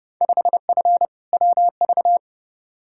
Morse code, heard as a keyed tone, about 30 wpm, 700 Hz. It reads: 5F WV
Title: Field Day Exchanges Encore - 30wpm